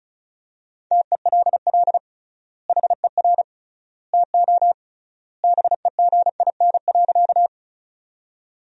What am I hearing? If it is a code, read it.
Morse code, heard as a keyed tone, about 36 wpm, 700 Hz. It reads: TELL HER TO BEGIN.
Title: 3x Word-Spacing - Sentences from Top 300 Words - 35wpm